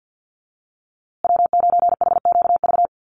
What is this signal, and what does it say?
Morse code, heard as a keyed tone, about 50 wpm, 700 Hz. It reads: W9HQV
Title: USA Call Signs Repeated 2x 50wpm